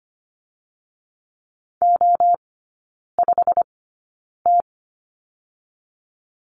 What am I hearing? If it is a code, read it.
Morse code, heard as a keyed tone, about 25 wpm, 700 Hz. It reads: O 5 T
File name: course / 25:10wpm / 100 Lesson 17e - Letter u - Three Characters 25wpm